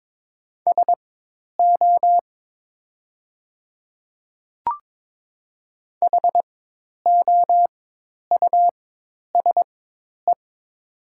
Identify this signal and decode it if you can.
Morse code, heard as a keyed tone, about 22 wpm, spaced as for 10 wpm, 700 Hz. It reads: SO HOUSE
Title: 4x Character Spacing - Top 200 Words - 22wpm